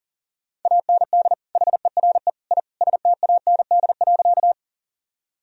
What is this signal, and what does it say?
Morse code, heard as a keyed tone, about 40 wpm, 700 Hz. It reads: AND HERE I STAND.
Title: Sentences from Top 1000 Words - Review Part 1 - 40wpm